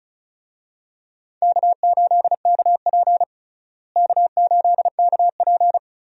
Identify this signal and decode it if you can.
Morse code, heard as a keyed tone, about 36 wpm, 700 Hz. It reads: K8KP K8KP
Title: USA Call Signs - 4 Characters - Repeated 3x - 35wpm